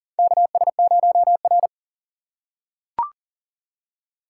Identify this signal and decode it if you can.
Morse code, heard as a keyed tone, about 40 wpm, 700 Hz. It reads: KS0R
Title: USA Call Signs - 4 Characters - 2x Character Spacing - 40wpm